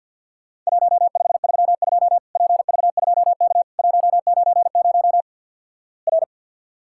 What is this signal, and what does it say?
Morse code, heard as a keyed tone, about 48 wpm, 700 Hz. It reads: 1532 PV2K 190 R
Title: Worked All Europe DX Contest - QTC Exchange - 50wpm